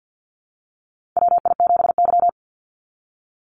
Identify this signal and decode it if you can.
Morse code, heard as a keyed tone, about 50 wpm, 700 Hz. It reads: WI7Y
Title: USA Call Signs 50wpm